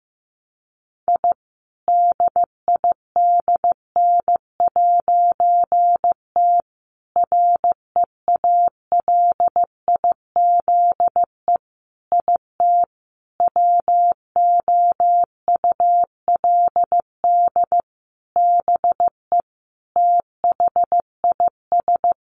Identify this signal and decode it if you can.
Morse code, heard as a keyed tone, about 14 wpm, 700 Hz. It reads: I DIDN'T REALIZE IT WOULD BE THIS
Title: A Sound Of Thunder 15wpm